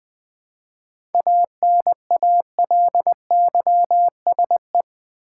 Morse code, heard as a keyed tone, about 20 wpm, 700 Hz. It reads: ANALYSE
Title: 7 Letter Words - First 3 Letters Given - 20wpm